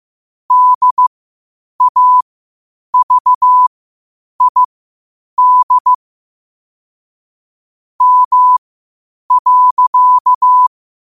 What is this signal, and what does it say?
Morse code, heard as a keyed tone, about 15 wpm, spaced as for 8 wpm, 1000 Hz. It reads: DAVID M.